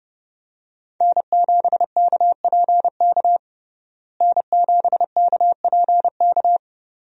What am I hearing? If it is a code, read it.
Morse code, heard as a keyed tone, about 32 wpm, 700 Hz. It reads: N7KPK N7KPK
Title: USA Call Signs - 5 Characters - Repeated 2x - 30wpm